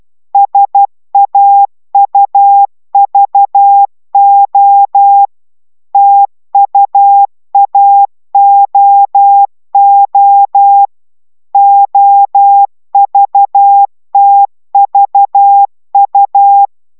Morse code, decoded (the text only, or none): SAUVO TUAOO OVTVU